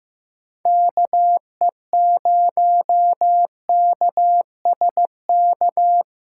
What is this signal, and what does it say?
Morse code, heard as a keyed tone, about 15 wpm, 700 Hz. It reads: KE0KSK
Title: USA Callsign DE Callsign 15wpm